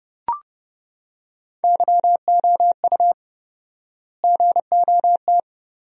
Morse code, heard as a keyed tone, about 30 wpm, 700 Hz. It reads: YOU GOT
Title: 4x Word-Spacing - Sentences from Top 300 Words - 30wpm